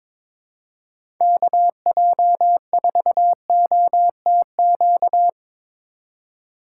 Morse code, heard as a keyed tone, about 22 wpm, 700 Hz. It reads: KJ4OTQ
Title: USA Call Signs - 6 Characters - 8x Character Spacing - 22wpm